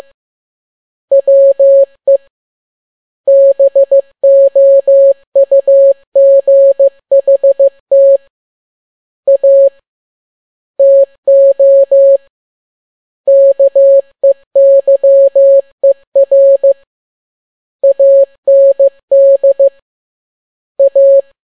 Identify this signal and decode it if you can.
Morse code, heard as a keyed tone, about 15 wpm, 550 Hz. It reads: WE BOUGHT A TO KEYER AND A
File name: n6tt p1 15C13